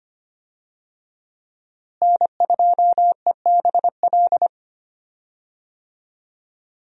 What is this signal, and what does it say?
Morse code, heard as a keyed tone, about 25 wpm, 700 Hz. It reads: N2EBL
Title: USA Call Signs - 5 Characters - Repeated 3x - 25wpm